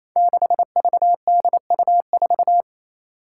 Morse code, heard as a keyed tone, about 28 wpm, 700 Hz. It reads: 6VDU4
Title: Five Challenging Letter-Number Combinations 28wpm